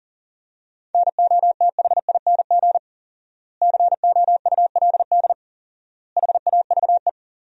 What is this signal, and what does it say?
Morse code, heard as a keyed tone, about 40 wpm, 700 Hz. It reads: NOTHING COULD HAVE
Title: 4x Word-Spacing - Sentences from Top 500 Words - 40wpm